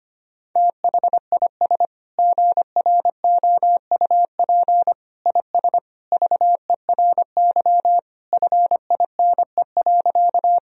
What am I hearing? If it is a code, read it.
Morse code, heard as a keyed tone, about 25 wpm, 700 Hz. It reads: THIS GROUP IS VERY FINE.
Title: Sentences from Top 1000 Words - Review Part 1 - 25wpm